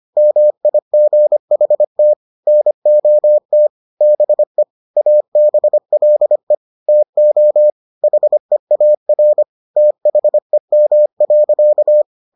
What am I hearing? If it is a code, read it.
Morse code, heard as a keyed tone, about 25 wpm, 600 Hz. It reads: MIGHT NOT BE ABLE TO HEAR THEM.